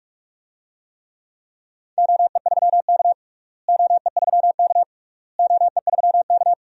Morse code, heard as a keyed tone, about 47 wpm, 700 Hz. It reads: OE3K OE3K OE3K
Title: DXCC-100 Call Signs - 4 Characters - Repeated 3x - 45wpm